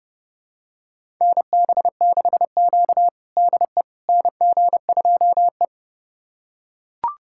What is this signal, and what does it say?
Morse code, heard as a keyed tone, about 30 wpm, 700 Hz. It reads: NB6Q DE NG2E
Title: USA Callsign DE Callsign - 4 Characters - 30wpm